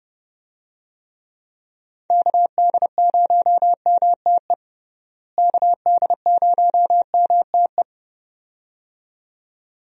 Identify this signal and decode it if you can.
Morse code, heard as a keyed tone, about 30 wpm, 700 Hz. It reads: KD0MTE KD0MTE
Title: USA Call Signs - 6 Characters - Repeated 2x - 30wpm